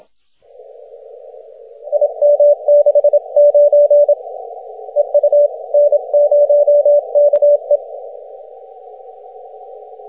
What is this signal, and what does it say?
Morse code, heard as a keyed tone, about 27 wpm, 580 Hz. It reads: IM69 EU N0KE
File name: wp / n0ke 6m cw june 7th 2010